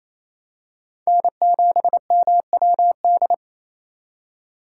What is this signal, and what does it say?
Morse code, heard as a keyed tone, about 28 wpm, 700 Hz. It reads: N7MWD
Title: USA Call Signs - 5 Characters - 2x Character Spacing - 28wpm